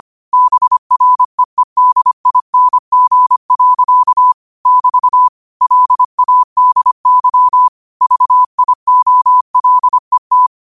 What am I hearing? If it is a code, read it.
Morse code, heard as a keyed tone, about 25 wpm, 1000 Hz. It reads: BREEDING. <BT> LADY VIOLET
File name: qotdfiles / 25wpm